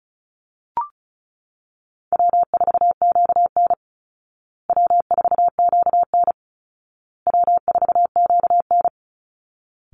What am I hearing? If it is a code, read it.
Morse code, heard as a keyed tone, about 34 wpm, 700 Hz. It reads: W4QN W4QN W4QN
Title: Easy Call Signs Repeated 3x - 35wpm